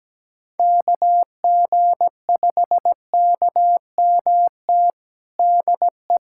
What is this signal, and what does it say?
Morse code, heard as a keyed tone, about 17 wpm, 700 Hz. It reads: KG5KMT DE
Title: USA Callsign DE Callsign - 6 Characters - 17wpm